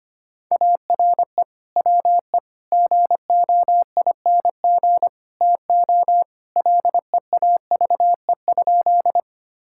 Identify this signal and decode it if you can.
Morse code, heard as a keyed tone, about 25 wpm, 700 Hz. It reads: ARE WE GOING TO LEAVE?
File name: practice / 25wpm / Sentences with 4-Word Phrases - 25wpm